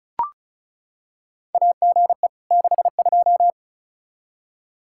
Morse code, heard as a keyed tone, about 36 wpm, 700 Hz. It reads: AGE 62
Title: Practice QSO - Exchange 3 - More Info - 35wpm